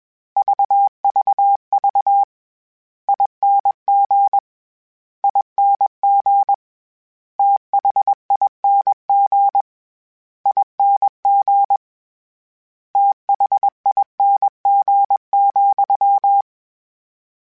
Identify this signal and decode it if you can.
Morse code, heard as a keyed tone, about 21 wpm, 800 Hz. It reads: VVV ING ING THING ING THING,
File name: Special INGs-mp3s / Special INGs-mp3s / VerySlowToFaster Ing @20